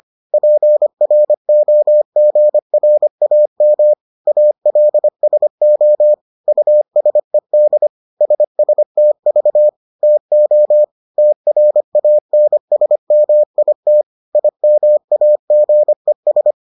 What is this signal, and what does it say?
Morse code, heard as a keyed tone, about 25 wpm, 600 Hz. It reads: PROGRAM ALSO USED SSTV TO TRANSMIT IMAGES